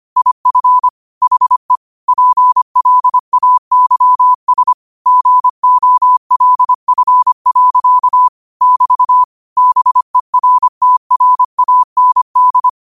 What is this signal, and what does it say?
Morse code, heard as a keyed tone, about 25 wpm, 1000 Hz. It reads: IF HE PLAYS GOLF. <BT> BERTRAND
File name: qotdfiles / 25wpm